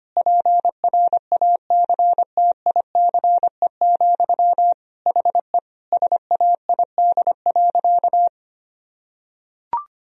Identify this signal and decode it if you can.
Morse code, heard as a keyed tone, about 25 wpm, 700 Hz. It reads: PRACTICE, HE SAID.
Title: Sentences from Top 1000 Words - Review Part 4 - 25wpm